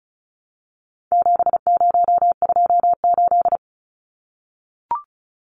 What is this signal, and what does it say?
Morse code, heard as a keyed tone, about 34 wpm, 700 Hz. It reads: 7028
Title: Numbers - 4-Digit 35wpm